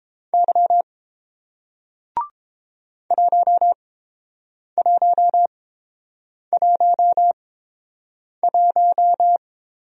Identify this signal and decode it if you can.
Morse code, heard as a keyed tone, about 30 wpm, spaced as for 7 wpm, 700 Hz. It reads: Y 1111